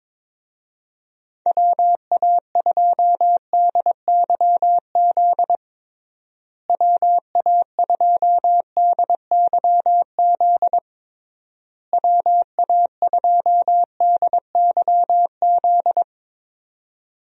Morse code, heard as a keyed tone, about 22 wpm, 700 Hz. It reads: WA2DYZ WA2DYZ WA2DYZ
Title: USA Call Signs - 6 Characters - Repeated 3x - 22wpm